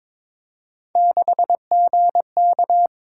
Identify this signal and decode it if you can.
Morse code, heard as a keyed tone, about 22 wpm, 700 Hz. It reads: 6GK
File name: practice / 22wpm / Three Letter-Number Combinations - 22wpm